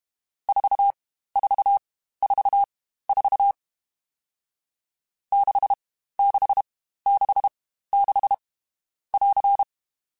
Morse code, heard as a keyed tone, about 32 wpm, spaced as for 15 wpm, 800 Hz. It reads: 4444 6666 <AR>
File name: Short Runs / Short Runs-mp3s / Staccato@30